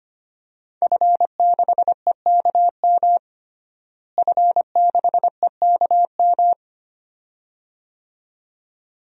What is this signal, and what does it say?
Morse code, heard as a keyed tone, about 25 wpm, 700 Hz. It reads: F6EKM F6EKM
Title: DXCC-100 Call Signs - 5 Characters - Repeated 2x - 25wpm